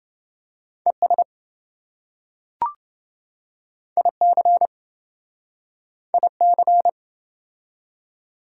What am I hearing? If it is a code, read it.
Morse code, heard as a keyed tone, about 26 wpm, 700 Hz. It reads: ES IC IC